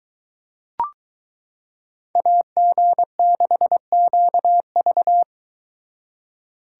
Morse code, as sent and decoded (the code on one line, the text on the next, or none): .- --. -.... --.- ...-
AG6QV